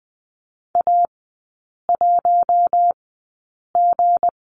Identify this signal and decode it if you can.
Morse code, heard as a keyed tone, about 20 wpm, spaced as for 8 wpm, 700 Hz. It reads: A1G
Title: 166 Lesson 27e - Number 7 - Three Characters 20wpm